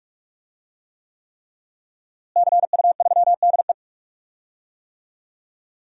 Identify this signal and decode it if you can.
Morse code, heard as a keyed tone, about 43 wpm, 700 Hz. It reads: CU3DE
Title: DXCC-100 Call Signs - 5 Characters - Repeated 3x - 45wpm